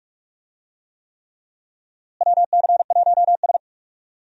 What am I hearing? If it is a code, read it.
Morse code, heard as a keyed tone, about 46 wpm, 700 Hz. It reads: WC1S
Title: USA Call Signs - 4 Characters - Repeated 3x - 45wpm